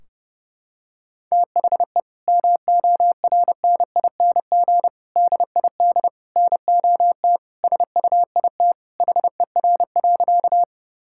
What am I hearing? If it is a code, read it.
Morse code, heard as a keyed tone, about 30 wpm, 700 Hz. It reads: THE MORNING DID NOT SUIT HER.
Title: Sentences from Top 1600 Words 30wpm